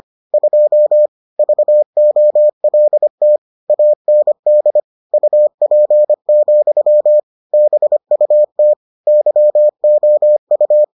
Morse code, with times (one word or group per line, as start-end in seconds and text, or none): 0.3-1.1 2
1.4-3.4 VOLT
3.7-4.8 AND
5.1-7.2 UP,
7.5-8.7 BUT
9.1-10.8 YOU